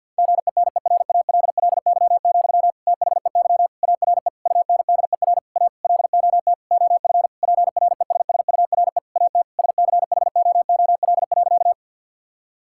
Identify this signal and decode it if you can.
Morse code, heard as a keyed tone, about 49 wpm, 700 Hz. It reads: GENERALLY, THEY ARE UNDER A LOT OF PRESSURE AT SCHOOL.
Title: Sentences from Top 1000 Words - Review Part 6 - 50wpm